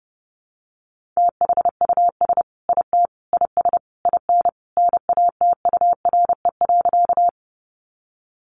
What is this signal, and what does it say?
Morse code, heard as a keyed tone, about 30 wpm, 700 Hz. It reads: THUS IT IS IN NATURE.
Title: Sentences from Top 800 Words 30wpm